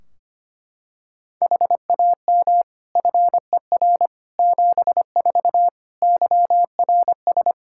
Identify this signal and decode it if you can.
Morse code, heard as a keyed tone, about 25 wpm, 700 Hz. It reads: HAM FER 74 YRS
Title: Practice QSO - Exchange 3 - More Info - 25wpm